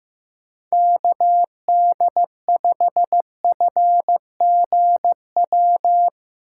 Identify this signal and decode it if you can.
Morse code, heard as a keyed tone, about 15 wpm, 700 Hz. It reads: KD5FGW
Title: USA Call Signs - 6 Characters - 4x Character Spacing - 15wpm